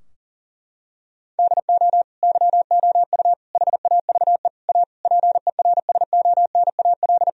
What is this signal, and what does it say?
Morse code, heard as a keyed tone, about 40 wpm, 700 Hz. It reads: DO YOU HAVE A PERSONAL
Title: Sentences from Top 1000 Words - Review Part 4 - 40wpm